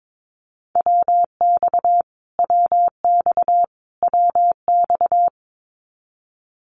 Morse code, heard as a keyed tone, about 22 wpm, 700 Hz. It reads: WX WX WX
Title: Abbreviations Repeated 3x - 22wpm